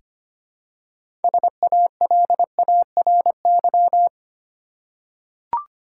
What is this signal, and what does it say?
Morse code, heard as a keyed tone, about 25 wpm, 700 Hz. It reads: SALARY
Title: Six Letter Words - 25wpm